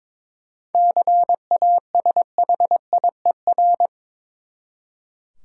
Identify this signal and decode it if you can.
Morse code, heard as a keyed tone, about 22 wpm, 700 Hz. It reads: CASHIER
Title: Practice QSO - Exchange 3 - More Info - 22wpm